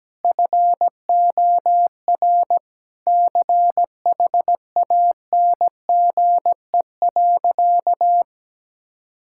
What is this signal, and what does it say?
Morse code, heard as a keyed tone, about 17 wpm, 700 Hz. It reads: FOR CHANGE.
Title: Sentences from Top 1000 Words - Review Part 6 - 17wpm